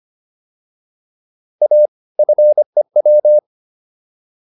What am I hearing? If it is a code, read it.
Morse code, heard as a keyed tone, about 25 wpm, 600 Hz. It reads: A FEW